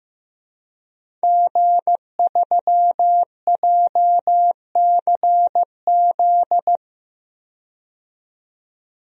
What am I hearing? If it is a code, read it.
Morse code, heard as a keyed tone, about 15 wpm, 700 Hz. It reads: G3JCZ
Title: DXCC-100 Call Signs - 5 Characters - 15wpm